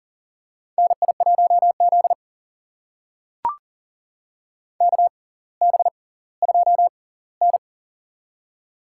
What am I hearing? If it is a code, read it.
Morse code, heard as a keyed tone, about 40 wpm, spaced as for 20 wpm, 700 Hz. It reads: NI1Z KB2N
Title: USA Call Signs - 4 Characters - 6x Character Spacing - 40wpm